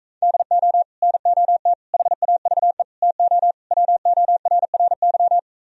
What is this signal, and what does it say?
Morse code, heard as a keyed tone, about 38 wpm, 700 Hz. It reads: DO NOT HAVE TO WORRY